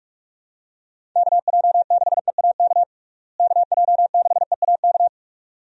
Morse code, heard as a keyed tone, about 44 wpm, 700 Hz. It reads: KJ6EAK KJ6EAK
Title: USA Call Signs - 6 Characters - Repeated 2x - 45wpm